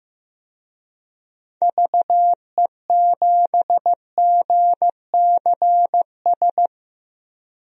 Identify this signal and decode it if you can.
Morse code, heard as a keyed tone, about 15 wpm, 700 Hz. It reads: VE7GCS
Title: DXCC-100 Call Signs - 6 Characters - 15wpm